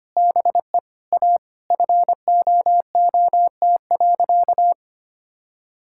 Morse code, heard as a keyed tone, about 25 wpm, 700 Hz. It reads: BE A FOOT.
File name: practice / 25wpm / All-Age Appropriate Jokes - 25wpm